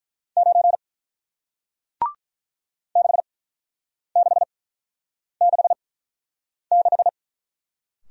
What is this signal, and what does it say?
Morse code, heard as a keyed tone, about 47 wpm, spaced as for 9 wpm, 700 Hz. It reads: 9 6666